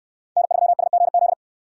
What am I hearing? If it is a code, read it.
Morse code, heard as a keyed tone, about 67 wpm, 700 Hz. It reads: N3SGZ